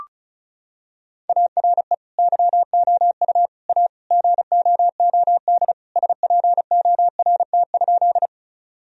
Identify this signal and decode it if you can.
Morse code, heard as a keyed tone, about 34 wpm, 700 Hz. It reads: ARE YOU A GOOD SPORT?
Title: Sentences from Top 1000 Words - Review Part 4 - 35wpm